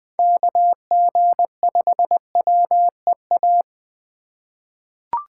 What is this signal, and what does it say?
Morse code, heard as a keyed tone, about 20 wpm, 700 Hz. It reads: KG5WEA
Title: USA Call Signs - 6 Characters - 4x Character Spacing - 20wpm